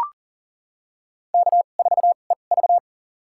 Morse code, heard as a keyed tone, about 39 wpm, spaced as for 29 wpm, 700 Hz. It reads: K4EV
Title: USA Call Signs - 4 Characters - 2x Character Spacing - 40wpm